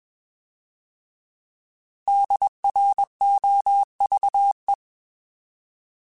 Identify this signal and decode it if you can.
Morse code, heard as a keyed tone, about 20 wpm, 800 Hz. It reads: DROVE
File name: Alphabetical / Alphabetical mp3s / Ds @20